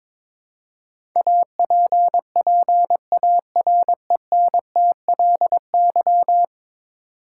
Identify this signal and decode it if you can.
Morse code, heard as a keyed tone, about 22 wpm, 700 Hz. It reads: APPARENTLY
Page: morse-code-ninja-practice-22wpm